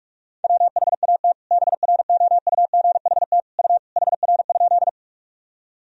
Morse code, heard as a keyed tone, about 43 wpm, 700 Hz. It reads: WHAT BROUGHT U HR?
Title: Sentences from Top 500 Words and Abbreviations 45wpm